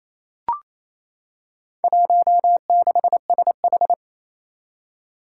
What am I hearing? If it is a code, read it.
Morse code, heard as a keyed tone, about 28 wpm, 700 Hz. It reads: 16SH
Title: Four Challenging Letter-Number Combinations 28wpm